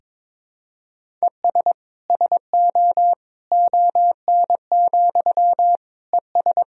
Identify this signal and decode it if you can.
Morse code, heard as a keyed tone, about 22 wpm, 700 Hz. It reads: ES SO ON, ES
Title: Sentences from Top 100 Words and Abbreviations 22wpm